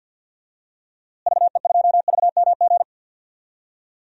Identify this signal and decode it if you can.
Morse code, heard as a keyed tone, about 53 wpm, 700 Hz. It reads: VE2VMG
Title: DXCC-100 Call Signs - 6 Characters - Repeated 2x - 50wpm